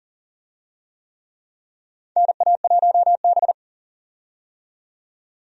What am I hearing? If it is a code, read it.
Morse code, heard as a keyed tone, about 40 wpm, 700 Hz. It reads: NA1B
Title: USA Call Signs - 4 Characters - 40wpm